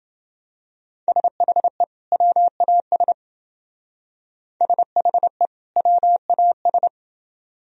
Morse code, heard as a keyed tone, about 28 wpm, 700 Hz. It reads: SHE WAS SHE WAS